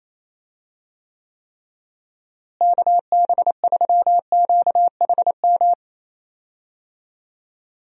Morse code, heard as a keyed tone, about 27 wpm, 700 Hz. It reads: KB3QHM
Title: USA Call Signs - 6 Characters - Repeated 3x - 28wpm